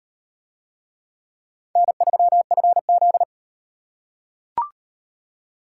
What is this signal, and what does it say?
Morse code, heard as a keyed tone, about 36 wpm, 700 Hz. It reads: N3FZ